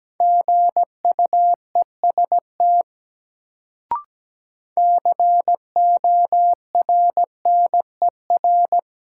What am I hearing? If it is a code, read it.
Morse code, heard as a keyed tone, about 17 wpm, 700 Hz. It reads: GUEST CORNER
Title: Sentences from Top 1600 Words - New Word Drill Alt - 17wpm